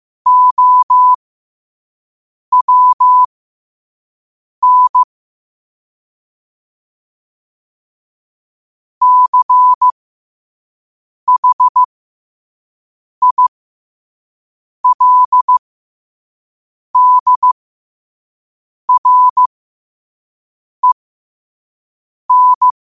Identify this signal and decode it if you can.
Morse code, heard as a keyed tone, about 15 wpm, spaced as for 5 wpm, 1000 Hz. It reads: OWN CHILDREN
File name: qotdfiles / 5wpm